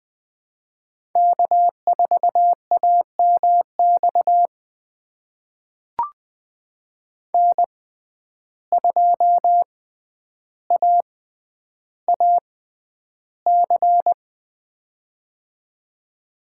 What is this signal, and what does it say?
Morse code, heard as a keyed tone, about 20 wpm, spaced as for 10 wpm, 700 Hz. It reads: K4AMX N2AAC
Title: USA Call Signs - 5 Characters - 6x Character Spacing - 20wpm